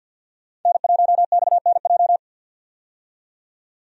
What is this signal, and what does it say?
Morse code, heard as a keyed tone, about 50 wpm, 700 Hz. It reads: N1XNJ
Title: USA Call Signs - 5 Characters - 50wpm